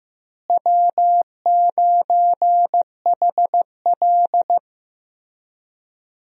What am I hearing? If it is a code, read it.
Morse code, heard as a keyed tone, about 15 wpm, 700 Hz. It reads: W9HL
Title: USA Call Signs - 4 Characters - Repeated 2x - 15wpm